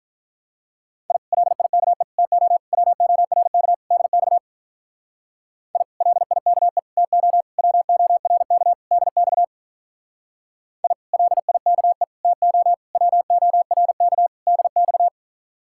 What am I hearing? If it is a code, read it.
Morse code, heard as a keyed tone, about 43 wpm, 700 Hz. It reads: I LIKE TO WORK DX I LIKE TO WORK DX I LIKE TO WORK DX
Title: Speed-Racing Common QSO Phrases - 35wpm